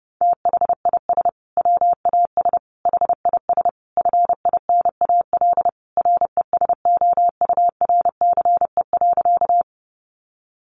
Morse code, heard as a keyed tone, about 30 wpm, 700 Hz. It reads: THIS WAS HIS FINAL RESOURCE.
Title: Sentences from Top 1300 Words 30wpm